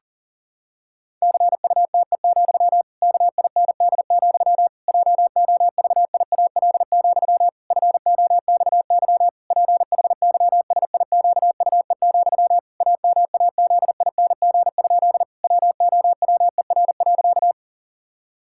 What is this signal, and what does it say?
Morse code, heard as a keyed tone, about 37 wpm, 700 Hz. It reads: CUTE, KIND, JOVIAL, FOXY PHYSIQUE, AMAZING? WOWER.
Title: Pangram Challenge v2 - 40wpm